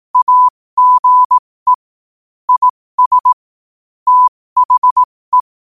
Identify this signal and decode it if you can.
Morse code, heard as a keyed tone, about 18 wpm, spaced as for 15 wpm, 1000 Hz. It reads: AGE IS THE
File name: qotdfiles / 15wpm